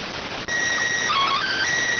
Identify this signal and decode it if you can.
SSTV (slow-scan television), PD 290.